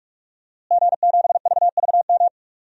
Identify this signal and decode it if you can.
Morse code, heard as a keyed tone, about 46 wpm, 700 Hz. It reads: G7VVM
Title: DXCC-100 Call Signs - 5 Characters - Repeated 3x - 45wpm